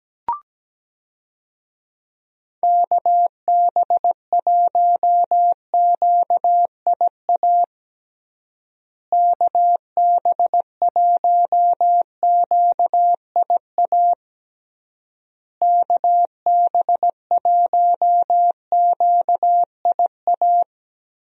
Morse code, heard as a keyed tone, about 17 wpm, 700 Hz. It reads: KB1QIA KB1QIA KB1QIA
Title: USA Call Signs - 6 Characters - Repeated 3x - 17wpm